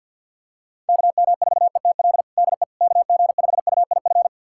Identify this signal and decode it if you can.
Morse code, heard as a keyed tone, about 52 wpm, 700 Hz. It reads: KM4ETL DE KG5UIF
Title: USA Callsign DE Callsign - 6 Characters - 50wpm